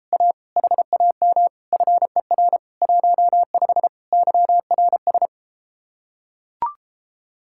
Morse code, heard as a keyed tone, about 33 wpm, 700 Hz. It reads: A HAM FER 15 YRS